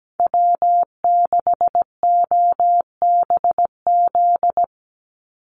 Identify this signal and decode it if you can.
Morse code, heard as a keyed tone, about 17 wpm, 700 Hz. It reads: W6OBZ
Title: USA Call Signs 17wpm